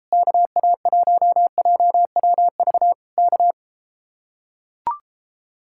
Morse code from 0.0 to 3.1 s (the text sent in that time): KA1JWV